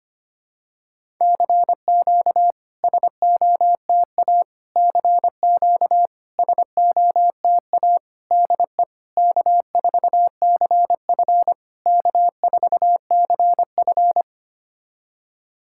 Morse code, heard as a keyed tone, about 25 wpm, 690 Hz. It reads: CQ SOTA CQ SOTA DE K4CF K4CF
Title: SOTA QSO Practice 25wpm